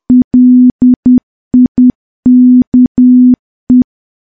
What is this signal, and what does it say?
Morse code, heard as a keyed tone, about 10 wpm, 250 Hz. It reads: LIKE
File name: morse / morse-training-10WPM